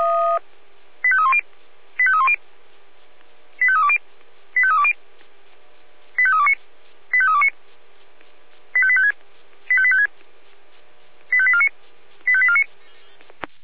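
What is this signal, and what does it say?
Five-tone selective call (ZVEI), heard as quick runs of five tones, tones of about 70 ms. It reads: 85310 85310 85320 85320 85329 85329 85856 85856 85849 85849
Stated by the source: Received with an SDR; ZVEI five-tone selective call.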